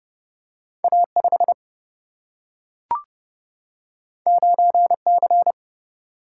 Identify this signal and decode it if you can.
Morse code, heard as a keyed tone, about 31 wpm, 700 Hz. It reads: A5 9C
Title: Single-Letter Single-Number 30wpm